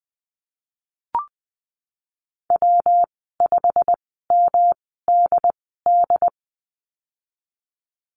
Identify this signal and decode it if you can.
Morse code, heard as a keyed tone, about 20 wpm, spaced as for 15 wpm, 700 Hz. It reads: W5MDD
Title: 122 Lesson 20g - Number 3 - Call Signs 20wpm